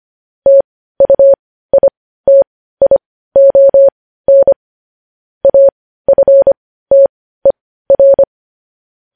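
Morse code, heard as a keyed tone, about 25 wpm, spaced as for 15 wpm, 550 Hz. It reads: TUITION AFTER
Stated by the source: Morse code news headlines sent at 15, preceded by vvv.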